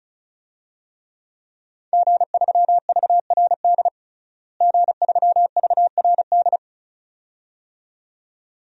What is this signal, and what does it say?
Morse code, heard as a keyed tone, about 36 wpm, 700 Hz. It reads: G3VRD G3VRD
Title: DXCC-100 Call Signs - 5 Characters - Repeated 2x - 35wpm